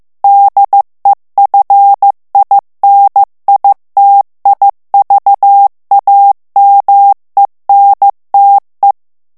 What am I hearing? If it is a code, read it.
Morse code, heard as a keyed tone, about 15 wpm, 800 Hz. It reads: DEFINITIVAMENTE
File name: morse mp3 / ITALIEN texte clair 1 vitesse 14 mots minute